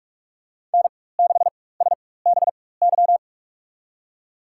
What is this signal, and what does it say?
Morse code, heard as a keyed tone, about 46 wpm, spaced as for 21 wpm, 700 Hz. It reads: N6SBY
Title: USA Call Signs - 5 Characters - 4x Character Spacing - 45wpm